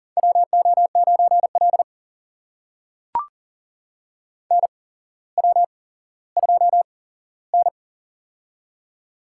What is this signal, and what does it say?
Morse code, heard as a keyed tone, about 40 wpm, spaced as for 17 wpm, 700 Hz. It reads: WO9L NW2N
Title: USA Call Signs - 4 Characters - 8x Character Spacing - 40wpm